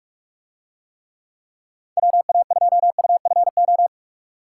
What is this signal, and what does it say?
Morse code, heard as a keyed tone, about 42 wpm, 700 Hz. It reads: WA2UFO